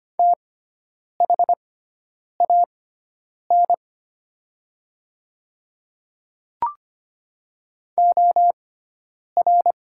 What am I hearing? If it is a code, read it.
Morse code, heard as a keyed tone, about 25 wpm, spaced as for 9 wpm, 700 Hz. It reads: THAN OR
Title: 6x Character Spacing - Top 100 Words - 25wpm